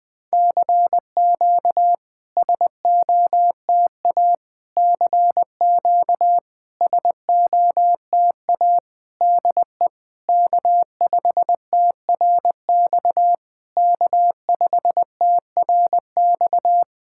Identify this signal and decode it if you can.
Morse code, heard as a keyed tone, about 21 wpm, 690 Hz. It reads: CQ SOTA CQ SOTA DE K5TRX K5TRX
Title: SOTA QSO Practice 20wpm